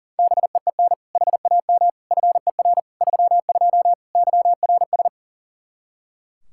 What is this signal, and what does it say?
Morse code, heard as a keyed tone, about 40 wpm, 700 Hz. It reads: BEEN HAM FER 32 YRS
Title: K1USN SST QSO Practice 40wpm